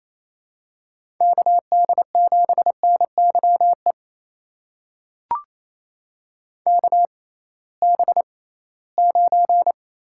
Morse code, heard as a keyed tone, about 28 wpm, spaced as for 18 wpm, 700 Hz. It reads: KD7NYE KB9
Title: USA Call Signs - 6 Characters - 6x Character Spacing - 28wpm